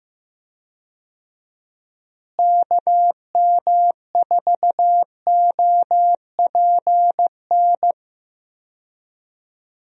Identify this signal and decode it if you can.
Morse code, heard as a keyed tone, about 15 wpm, 700 Hz. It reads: KM4OPN
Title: USA Call Signs - 6 Characters - Repeated 2x - 15wpm